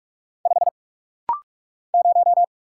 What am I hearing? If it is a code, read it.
Morse code, heard as a keyed tone, about 46 wpm, 700 Hz. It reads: 5 0
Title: Single Letter-Number Alt - Rapid-Fire 45wpm